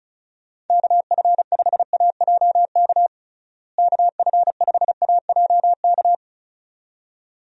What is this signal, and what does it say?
Morse code, heard as a keyed tone, about 36 wpm, 700 Hz. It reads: KF5AJK KF5AJK
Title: USA Call Signs - 6 Characters - Repeated 2x - 35wpm